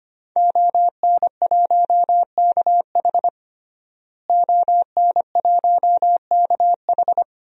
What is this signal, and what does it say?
Morse code, heard as a keyed tone, about 25 wpm, 700 Hz. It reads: ON1KH ON1KH
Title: DXCC-100 Call Signs - 5 Characters - Repeated 3x - 25wpm